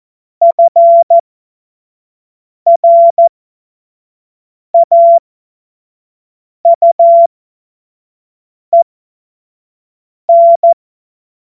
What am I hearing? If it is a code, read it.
Morse code, heard as a keyed tone, about 14 wpm, spaced as for 5 wpm, 680 Hz. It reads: FRAUEN
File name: Morsekurs / Witze / Witz 25. Januar 2022, 25 F70